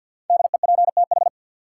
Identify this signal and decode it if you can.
Morse code, heard as a keyed tone, about 53 wpm, 700 Hz. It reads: DEPTH
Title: Sentences from Top 1200 Words - New Word Drill - 50wpm